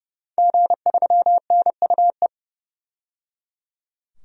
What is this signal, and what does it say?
Morse code, heard as a keyed tone, about 29 wpm, 700 Hz. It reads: G3NUE
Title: DXCC-100 Call Signs - 5 Characters - 30wpm